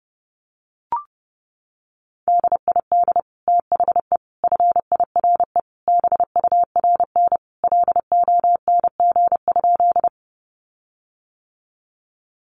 Morse code, heard as a keyed tone, about 30 wpm, 700 Hz. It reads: DID THE FIRE BURN LONG?
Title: Sentences from Top 1300 Words 30wpm